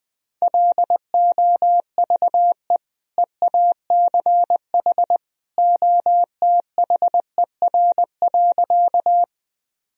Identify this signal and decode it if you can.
Morse code, heard as a keyed tone, about 20 wpm, 700 Hz. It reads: LOVE EACH OTHER.
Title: Sentences from Top 1000 Words - Review Part 1 - 20wpm